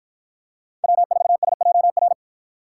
Morse code, heard as a keyed tone, about 51 wpm, 700 Hz. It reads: W4SJR